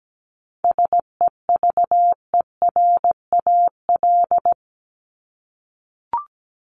Morse code, heard as a keyed tone, about 17 wpm, 700 Hz. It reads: SEVERAL
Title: Top 500 Words 17wpm